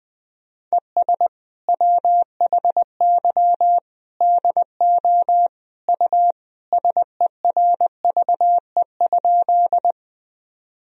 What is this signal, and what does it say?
Morse code, heard as a keyed tone, about 19 wpm, 700 Hz. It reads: ES WHY DO U SERVE?
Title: Sentences from Top 400 Words and Abbreviations 20wpm